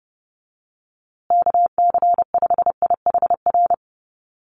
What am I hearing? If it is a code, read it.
Morse code, heard as a keyed tone, about 30 wpm, 700 Hz. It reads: KC5IHR
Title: USA Call Signs Repeated 3x 30wpm